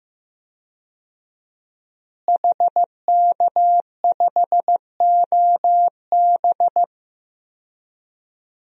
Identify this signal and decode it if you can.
Morse code, heard as a keyed tone, about 15 wpm, 700 Hz. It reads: HK5OB
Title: DXCC-100 Call Signs - 5 Characters - Repeated 2x - 15wpm